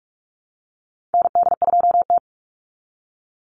Morse code, heard as a keyed tone, about 44 wpm, 700 Hz. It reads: ND2T
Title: Call Signs Repeated 2x - 45wpm